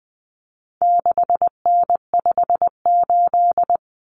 Morse code, heard as a keyed tone, about 20 wpm, 700 Hz. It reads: 6N58
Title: Common Cut Numbers - 4-Digit 20wpm